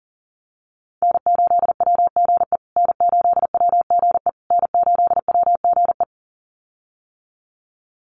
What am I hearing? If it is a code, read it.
Morse code, heard as a keyed tone, about 40 wpm, 700 Hz. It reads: N8WGE N8WGE N8WGE
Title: USA Call Signs Repeated 3x 40wpm